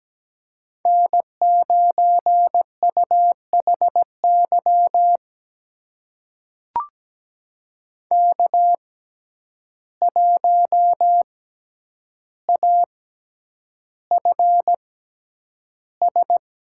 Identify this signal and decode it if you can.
Morse code, heard as a keyed tone, about 17 wpm, spaced as for 9 wpm, 700 Hz. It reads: N9UHY K1AFS
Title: USA Call Signs - 5 Characters - 6x Character Spacing - 17wpm